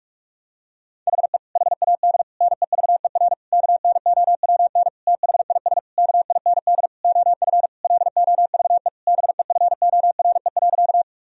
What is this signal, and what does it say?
Morse code, heard as a keyed tone, about 41 wpm, 700 Hz. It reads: HE HAD NEVER KNOWN THIS KIND OF LOVE BEFORE.
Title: Sentences from Top 1000 Words - Review Part 3 - 45wpm